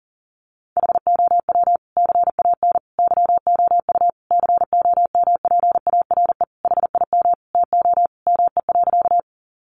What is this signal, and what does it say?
Morse code, heard as a keyed tone, about 40 wpm, 700 Hz. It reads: HOW CAN YOU COMPARE HIM TO ME.
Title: Sentences from Top 1300 Words 40wpm